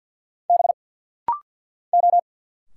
Morse code, heard as a keyed tone, about 50 wpm, 690 Hz. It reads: B O
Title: Single Letter-Number Alt - Rapid-Fire 50wpm